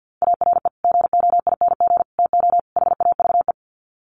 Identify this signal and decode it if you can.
Morse code, heard as a keyed tone, about 50 wpm, 700 Hz. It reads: ARE GOING TO HAVE
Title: Sets of 4 Words 50wpm